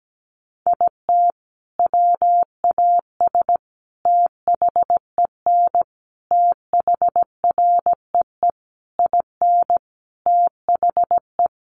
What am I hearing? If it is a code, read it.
Morse code, heard as a keyed tone, about 17 wpm, 700 Hz. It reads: IT WAS THEN THREE IN THE
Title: Sentences from Top 1500 Words 17wpm